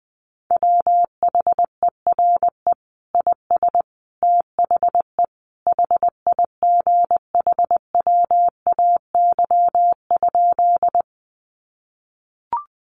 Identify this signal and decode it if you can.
Morse code, heard as a keyed tone, about 20 wpm, 700 Hz. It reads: WHERE IS THE HIGHWAY?
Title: Sentences from Top 1400 Words 20wpm